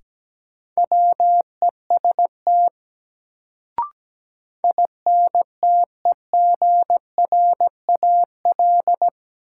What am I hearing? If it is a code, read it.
Morse code, heard as a keyed tone, about 17 wpm, 700 Hz. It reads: WEST INTEGRAL
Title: Sets of 5 Words - New Word Drill - 17wpm